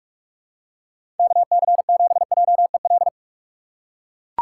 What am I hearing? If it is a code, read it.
Morse code, heard as a keyed tone, about 41 wpm, 700 Hz. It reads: KC7JEL
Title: USA Call Signs - 6 Characters - 4x Character Spacing - 45wpm